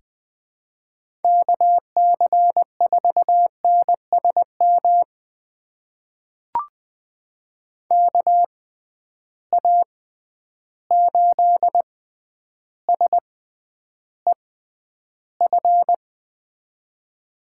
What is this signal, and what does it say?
Morse code, heard as a keyed tone, about 20 wpm, spaced as for 10 wpm, 700 Hz. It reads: KC4NSM KA8SEF
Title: USA Call Signs - 6 Characters - 6x Character Spacing - 20wpm